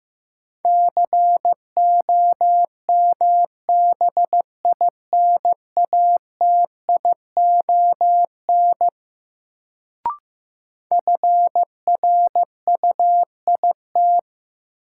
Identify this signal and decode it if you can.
Morse code, heard as a keyed tone, about 15 wpm, 700 Hz. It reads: COMBINATION FRUIT
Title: Sentences from Top 1300 Words - New Word Drill Alt - 15wpm